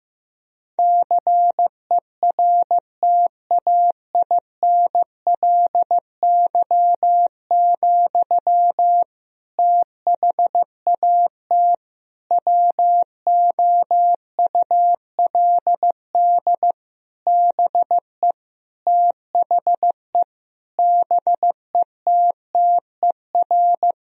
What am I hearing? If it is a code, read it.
Morse code, heard as a keyed tone, about 15 wpm, 700 Hz. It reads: CERTAINLY, THAT WOULD BE THE BETTER
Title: Sentences from Top 1000 Words - Review Part 2 - 15wpm